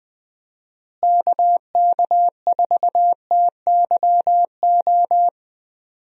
Morse code, heard as a keyed tone, about 20 wpm, 700 Hz. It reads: KK4TYO
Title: USA Call Signs - 6 Characters - Repeated 3x - 20wpm